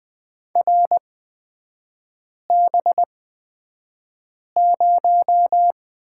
Morse code, heard as a keyed tone, about 20 wpm, spaced as for 5 wpm, 700 Hz. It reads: RB0
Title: 224 Lesson 36e - BT - Three Characters 20wpm